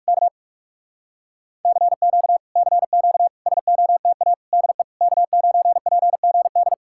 Morse code, heard as a keyed tone, about 59 wpm, 700 Hz. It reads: K CQ CQ SOTA DE K9PGD